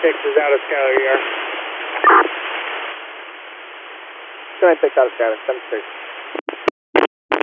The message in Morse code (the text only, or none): none